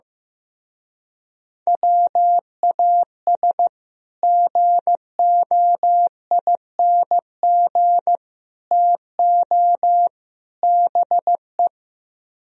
Morse code, heard as a keyed tone, about 15 wpm, 700 Hz. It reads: WAS GOING TO BE
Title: Speed-Racing - Sets of 4 Words 15wpm